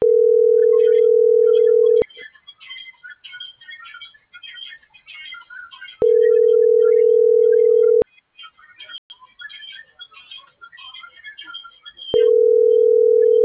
Ringback tone (North America), 440 + 480 Hz.